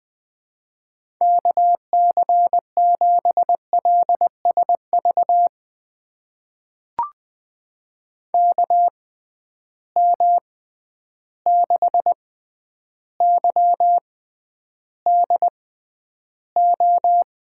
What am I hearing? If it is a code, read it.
Morse code, heard as a keyed tone, about 20 wpm, spaced as for 10 wpm, 700 Hz. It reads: KC7LSV KM6YDO
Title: USA Call Signs - 6 Characters - 6x Character Spacing - 20wpm